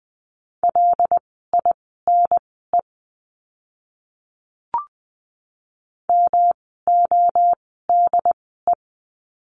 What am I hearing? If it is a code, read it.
Morse code, heard as a keyed tone, about 20 wpm, spaced as for 15 wpm, 700 Hz. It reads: LINE MODE
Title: 108 Lesson 18f - Letter m - Words 20wpm